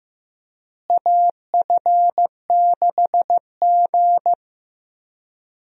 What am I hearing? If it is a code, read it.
Morse code, heard as a keyed tone, about 16 wpm, 700 Hz. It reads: AF6G
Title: USA Call Signs - 4 Characters - Repeated 2x - 15wpm